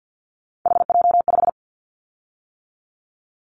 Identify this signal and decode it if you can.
Morse code, heard as a keyed tone, about 50 wpm, 700 Hz. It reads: HJ5
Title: Callsign Prefixes for Top 100 DXCC - 50wpm